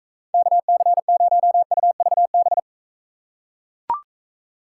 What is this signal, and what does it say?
Morse code, heard as a keyed tone, about 42 wpm, 700 Hz. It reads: KC0UVB